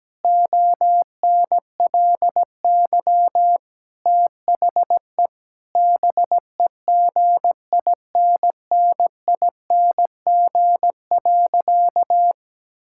Morse code, heard as a keyed tone, about 17 wpm, 700 Hz. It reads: ONLY THE BEGINNING.
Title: Sentences from Top 1000 Words - Review Part 5 - 17wpm